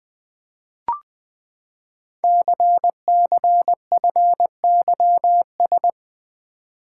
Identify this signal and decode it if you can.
Morse code, heard as a keyed tone, about 20 wpm, 700 Hz. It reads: CCFYS
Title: Code Groups - 5 Letters - 20wpm